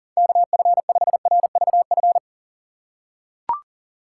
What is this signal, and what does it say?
Morse code, heard as a keyed tone, about 40 wpm, 700 Hz. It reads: KF5RVF
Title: USA Call Signs - 3x Character Spacing - 40wpm